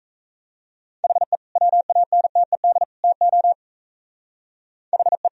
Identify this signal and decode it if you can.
Morse code, heard as a keyed tone, about 34 wpm, 700 Hz. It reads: HE WANTED TO HE
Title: Speed-Racing - Sets of 3 Words 28wpm